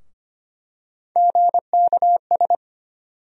Morse code, heard as a keyed tone, about 25 wpm, 700 Hz. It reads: GKS
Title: USA Call Sign Postfixes - 3 Letters 25wpm